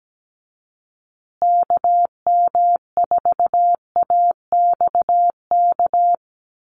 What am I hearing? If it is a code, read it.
Morse code, heard as a keyed tone, about 17 wpm, 700 Hz. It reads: KM4AXK
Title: USA Call Signs Repeated 3x 17wpm